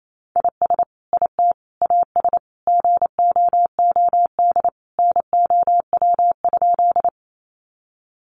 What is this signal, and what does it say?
Morse code, heard as a keyed tone, about 28 wpm, 700 Hz. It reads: IS IT AS GOOD NOW?
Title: Sentences from Top 200 Words 28wpm